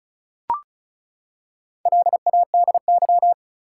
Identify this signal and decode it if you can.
Morse code, heard as a keyed tone, about 36 wpm, 700 Hz. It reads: LADY
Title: Four Letter Words - 35wpm